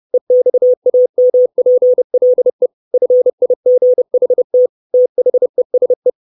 Morse code, heard as a keyed tone, about 30 wpm, 500 Hz. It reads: EXAMPLE FIGHT THESE
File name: 632 Words 30WPM